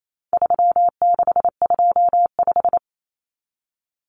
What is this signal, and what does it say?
Morse code, heard as a keyed tone, about 28 wpm, 700 Hz. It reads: 3625
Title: Common Cut Numbers - 4-Digit 28wpm